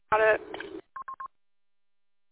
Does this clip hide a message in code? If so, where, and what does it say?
No, only speech.